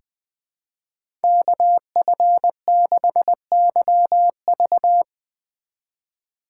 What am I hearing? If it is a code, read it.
Morse code, heard as a keyed tone, about 20 wpm, 700 Hz. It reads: KF6YV
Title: USA Call Signs - 2x Character Spacing - 20wpm